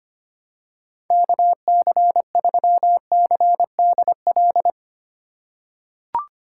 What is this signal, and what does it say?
Morse code, heard as a keyed tone, about 25 wpm, 700 Hz. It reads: KC3CDL
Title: USA Call Signs - 6 Characters - 6x Character Spacing - 25wpm